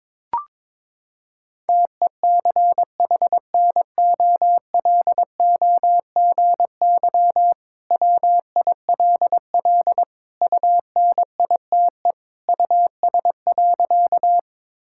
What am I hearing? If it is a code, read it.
Morse code, heard as a keyed tone, about 22 wpm, 700 Hz. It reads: TECHNOLOGY WILL UNITE US.
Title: Sentences from Top 1000 Words - Review Part 5 - 22wpm